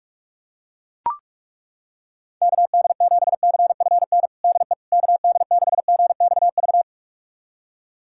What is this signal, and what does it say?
Morse code, heard as a keyed tone, about 41 wpm, 700 Hz. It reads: KD7CFN DE KD6GXV
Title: USA Callsign DE Callsign 45wpm